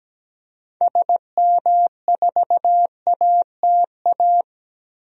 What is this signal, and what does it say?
Morse code, heard as a keyed tone, about 17 wpm, 700 Hz. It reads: SM4ATA
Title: DXCC-100 Call Signs - 6 Characters - 17wpm